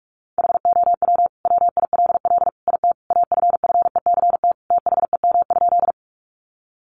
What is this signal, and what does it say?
Morse code, heard as a keyed tone, about 44 wpm, 700 Hz. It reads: HOW WILL IT AFFECT THEM?
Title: Sentences from Top 900 Words 45wpm